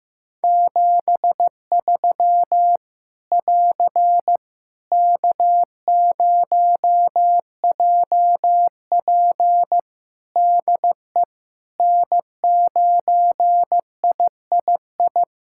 Morse code, heard as a keyed tone, about 15 wpm, 700 Hz. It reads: 73 <AR> K0JP DE N9III
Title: Practice QSO - Exchange 4 - Ending - 15wpm